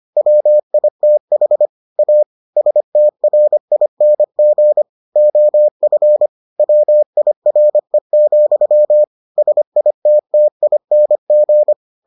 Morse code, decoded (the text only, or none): WITH A STRING OF WIRE, SITTING